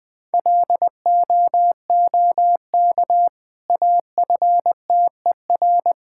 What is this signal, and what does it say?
Morse code, heard as a keyed tone, about 21 wpm, 700 Hz. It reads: LOOK AFTER
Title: Sentences from Top 1000 Words - Review Part 6 - 20wpm